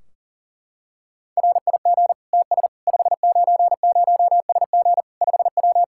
Morse code, heard as a keyed tone, about 40 wpm, 700 Hz. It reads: RIG TS 590SG 5W
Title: Practice QSO - Exchange 2 - Info - 40wpm